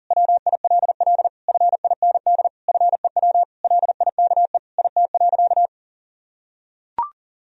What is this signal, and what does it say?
Morse code, heard as a keyed tone, about 40 wpm, 700 Hz. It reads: WILL FIND FEW LIKE IT.
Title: Sentences from Top 1000 Words - Review Part 1 - 40wpm